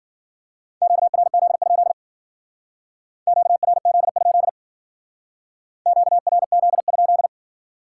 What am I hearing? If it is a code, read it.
Morse code, heard as a keyed tone, about 50 wpm, 700 Hz. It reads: QRZ? QRZ? QRZ?
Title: Speed-Racing Common QSO Phrases - 40wpm